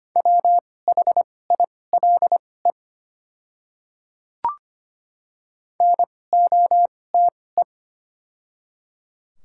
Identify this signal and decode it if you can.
Morse code, heard as a keyed tone, about 25 wpm, spaced as for 18 wpm, 690 Hz. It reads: WHILE NOTE
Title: 2x Character Spacing - Top 500 Words - 25wpm